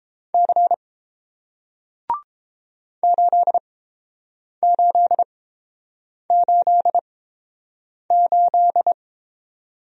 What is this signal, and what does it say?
Morse code, heard as a keyed tone, about 31 wpm, spaced as for 7 wpm, 700 Hz. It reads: C 8888